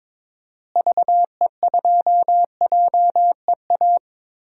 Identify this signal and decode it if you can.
Morse code, heard as a keyed tone, about 22 wpm, 700 Hz. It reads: VE2JEA
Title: DXCC-100 Call Signs - 6 Characters - Repeated 2x - 22wpm